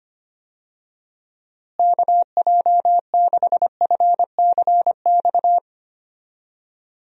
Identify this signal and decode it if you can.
Morse code, heard as a keyed tone, about 25 wpm, 700 Hz. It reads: KJ6FCX
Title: USA Call Signs - 6 Characters - Repeated 3x - 25wpm